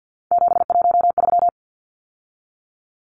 Morse code, heard as a keyed tone, about 50 wpm, 700 Hz. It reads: 713
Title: Numbers - 3-Digit 50wpm